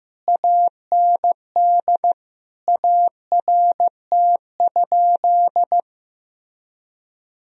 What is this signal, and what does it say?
Morse code, heard as a keyed tone, about 15 wpm, 700 Hz. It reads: AND ART?
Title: Sentences from Top 1000 Words - Review Part 3 - 15wpm